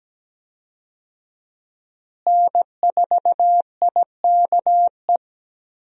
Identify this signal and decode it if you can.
Morse code, heard as a keyed tone, about 17 wpm, 700 Hz. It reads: N4IKE
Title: USA Call Signs - 5 Characters - Repeated 3x - 17wpm